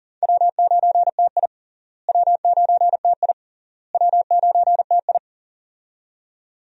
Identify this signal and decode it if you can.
Morse code, heard as a keyed tone, about 37 wpm, 700 Hz. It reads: W9TI W9TI W9TI
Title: USA Call Signs - 4 Characters - Repeated 3x - 40wpm